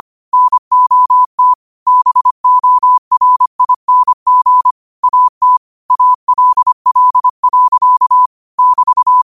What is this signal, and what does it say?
Morse code, heard as a keyed tone, about 25 wpm, 1000 Hz. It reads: NOT BORING AT ALL. <BT>
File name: qotdfiles / 25wpm